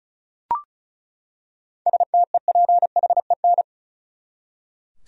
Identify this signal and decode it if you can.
Morse code, heard as a keyed tone, about 34 wpm, 700 Hz. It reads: STEPHEN
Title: USA Common Names - 35wpm